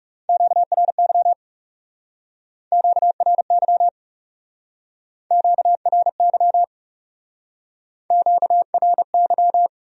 Morse code, heard as a keyed tone, about 39 wpm, 700 Hz. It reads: QRY QRY QRY QRY